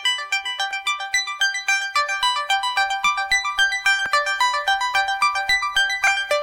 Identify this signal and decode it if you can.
Emergency Alert System (SAME) tones (header bursts), heard as harsh buzzing bursts. It is heard from 1.0 to 6.2 s.
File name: wrct-log hi- 16